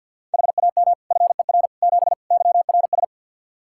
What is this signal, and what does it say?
Morse code, heard as a keyed tone, about 48 wpm, 700 Hz. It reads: HAM FER 7 YRS
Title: Practice QSO - Exchange 3 - More Info - 50wpm